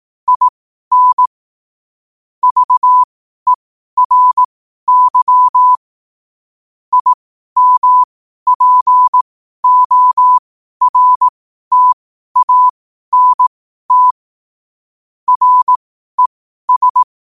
Morse code, heard as a keyed tone, about 18 wpm, spaced as for 12 wpm, 1000 Hz. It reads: IN VERY IMPORTANT RES
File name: qotdfiles / 12wpm